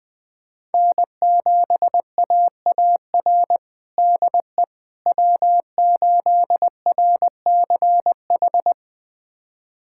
Morse code, heard as a keyed tone, about 20 wpm, 700 Hz. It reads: N7AAR DE W8RCH
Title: USA Callsign DE Callsign - 5 Characters - 20wpm